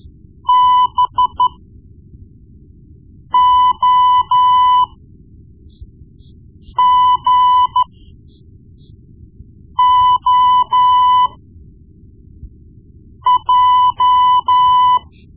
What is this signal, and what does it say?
Morse code, heard as a keyed tone, about 13 wpm, spaced as for 4 wpm, 1000 Hz. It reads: BOGOJ
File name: Morse Lessons / Morse Lessons / Tape-1B